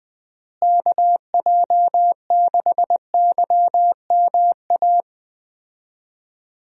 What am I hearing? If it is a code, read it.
Morse code, heard as a keyed tone, about 20 wpm, 700 Hz. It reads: KJ6YMA
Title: USA Call Signs - 6 Characters - Repeated 3x - 20wpm